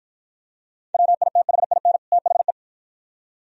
Morse code, heard as a keyed tone, about 53 wpm, 700 Hz. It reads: WITHIN THE